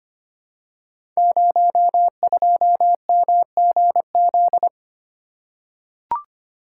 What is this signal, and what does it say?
Morse code, heard as a keyed tone, about 25 wpm, 690 Hz. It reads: 02MGZ
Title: Code Groups - 5 Letter:Numbers - 25wpm